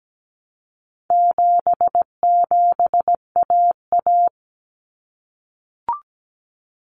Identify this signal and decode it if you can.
Morse code, heard as a keyed tone, about 17 wpm, 700 Hz. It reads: 77AA
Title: Common Cut Numbers - 4-Digit 17wpm